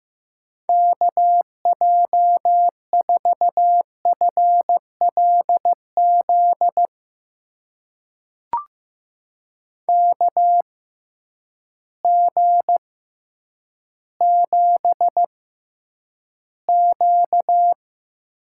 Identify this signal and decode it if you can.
Morse code, heard as a keyed tone, about 15 wpm, spaced as for 9 wpm, 700 Hz. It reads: KJ4FLZ KG7Q
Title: USA Call Signs - 6 Characters - 6x Character Spacing - 15wpm